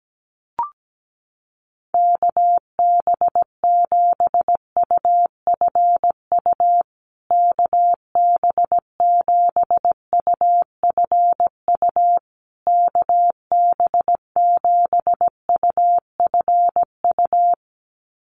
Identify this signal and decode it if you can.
Morse code, heard as a keyed tone, about 17 wpm, 700 Hz. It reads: KB7UFU KB7UFU KB7UFU
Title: USA Call Signs Repeated 3x 17wpm